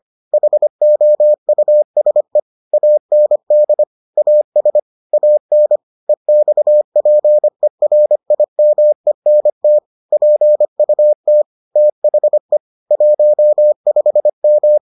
Morse code, heard as a keyed tone, about 25 wpm, 600 Hz. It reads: HOUSE AND AS AN EXPERIMENT PUT THE 15M